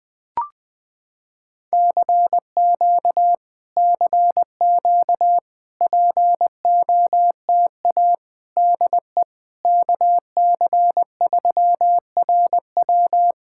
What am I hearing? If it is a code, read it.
Morse code, heard as a keyed tone, about 21 wpm, 700 Hz. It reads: CQ CQ POTA DE KC3RW
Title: POTA QSO Practice - 20wpm